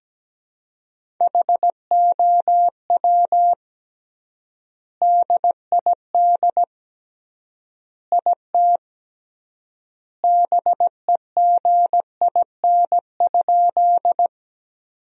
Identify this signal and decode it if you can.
Morse code, heard as a keyed tone, about 17 wpm, 700 Hz. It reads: HOW DID IT BEGIN?
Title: 3x Word-Spacing - Sentences from Top 300 Words - 17wpm